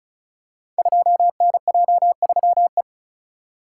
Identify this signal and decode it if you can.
Morse code, heard as a keyed tone, about 36 wpm, 700 Hz. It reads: 2NJ3E
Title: Code Groups - 5 Letter:Numbers - 35wpm